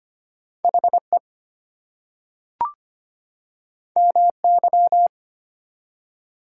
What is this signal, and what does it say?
Morse code, heard as a keyed tone, about 25 wpm, 700 Hz. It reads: HE MY
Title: Two Letter Words - 25wpm